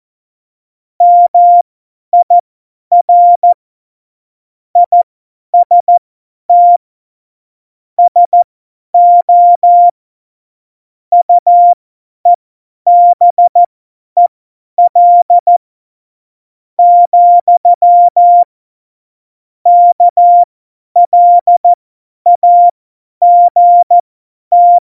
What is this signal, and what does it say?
Morse code, heard as a keyed tone, about 14 wpm, spaced as for 9 wpm, 700 Hz. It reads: MIR IST SO UEBEL, KLAGT
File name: Morsekurs / Witze / Witz 17.Juli 2018, 50 F70